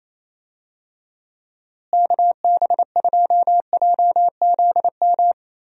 Morse code, heard as a keyed tone, about 28 wpm, 700 Hz. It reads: KB2JZM
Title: USA Call Signs - 6 Characters - Repeated 3x - 28wpm